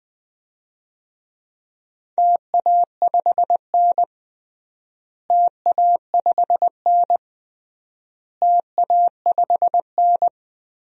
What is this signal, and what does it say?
Morse code, heard as a keyed tone, about 20 wpm, 700 Hz. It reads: TA5N TA5N TA5N
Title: DXCC-100 Call Signs - 4 Characters - Repeated 3x - 20wpm